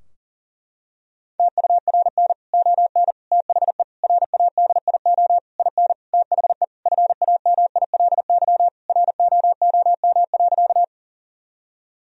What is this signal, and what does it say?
Morse code, heard as a keyed tone, about 40 wpm, 700 Hz. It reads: TURN ON THE RADIO IN THE FAMILY ROOM.
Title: Sentences from Top 1000 Words - Review Part 3 - 40wpm